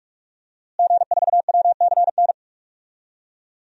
Morse code, heard as a keyed tone, about 46 wpm, 700 Hz. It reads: G4WCN
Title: DXCC-100 Call Signs - 5 Characters - Repeated 2x - 45wpm